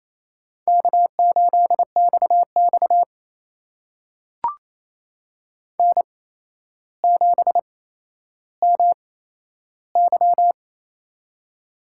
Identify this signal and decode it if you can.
Morse code, heard as a keyed tone, about 28 wpm, spaced as for 12 wpm, 700 Hz. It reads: K8XX N7MY
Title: USA Call Signs - 4 Characters - 8x Character Spacing - 28wpm